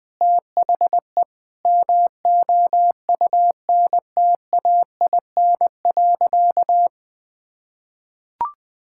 Morse code, heard as a keyed tone, about 21 wpm, 700 Hz. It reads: THE MOUNTAIN.
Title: Sentences from Top 1000 Words - Review Part 6 - 20wpm